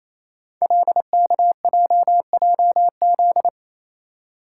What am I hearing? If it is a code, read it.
Morse code, heard as a keyed tone, about 27 wpm, 700 Hz. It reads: LKJJZ
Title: Code Groups - 5 Letters - 28wpm